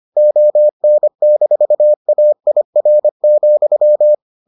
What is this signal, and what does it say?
Morse code, heard as a keyed tone, about 25 wpm, 600 Hz. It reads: ON-AIR,